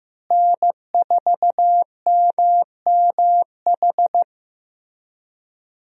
Morse code, heard as a keyed tone, about 16 wpm, 700 Hz. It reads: N4MMH
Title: USA Call Signs - 5 Characters - Repeated 3x - 15wpm